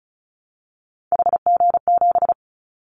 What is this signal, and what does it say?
Morse code, heard as a keyed tone, about 36 wpm, 690 Hz. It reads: HG7
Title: Callsign Prefixes for Top 25 DXCC - 35wpm